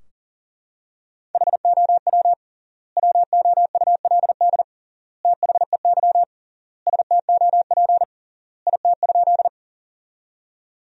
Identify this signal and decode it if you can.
Morse code, heard as a keyed tone, about 40 wpm, 700 Hz. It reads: HOW WOULD THEY STOP IT?
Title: 3x Word-Spacing - Sentences from Top 300 Words - 40wpm